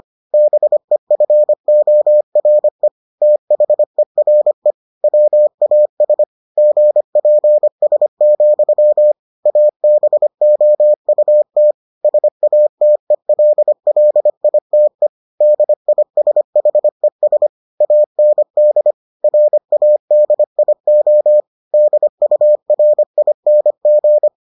Morse code, heard as a keyed tone, about 25 wpm, 600 Hz. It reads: BEFORE THERE WAS GPS, ABOUT SATELLITE DISHES AND RADIO DURING